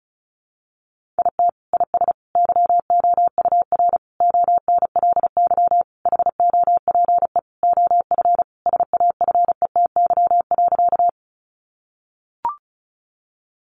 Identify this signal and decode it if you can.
Morse code, heard as a keyed tone, about 34 wpm, 700 Hz. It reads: IT IS YOUR ONLY HOPE OF SAFETY.
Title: Sentences from Top 1000 Words 35wpm